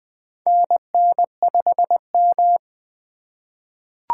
Morse code, heard as a keyed tone, about 20 wpm, 700 Hz. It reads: NN5M
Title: USA Call Signs - 4 Characters - 4x Character Spacing - 20wpm